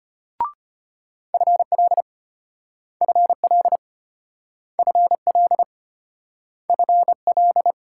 Morse code, heard as a keyed tone, about 32 wpm, 700 Hz. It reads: FL FL FL FL